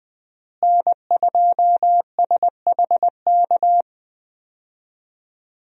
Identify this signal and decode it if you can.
Morse code, heard as a keyed tone, about 20 wpm, 700 Hz. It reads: N2SHK
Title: USA Call Signs - 5 Characters - Repeated 3x - 20wpm